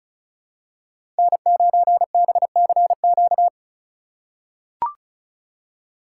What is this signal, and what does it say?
Morse code, heard as a keyed tone, about 36 wpm, 700 Hz. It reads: N9BCQ
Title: USA Call Signs - 5 Characters - 3x Character Spacing - 35wpm